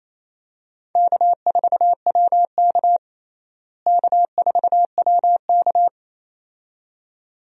Morse code, heard as a keyed tone, about 28 wpm, 700 Hz. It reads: K4WK K4WK
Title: USA Call Signs - 4 Characters - Repeated 2x - 28wpm